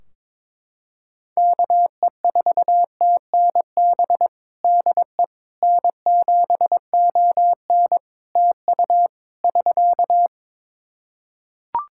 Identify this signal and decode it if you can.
Morse code, heard as a keyed tone, about 22 wpm, 700 Hz. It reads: KE4TNB DE N7ON TU <SK>
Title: Practice QSO - Exchange 4 - Ending - 22wpm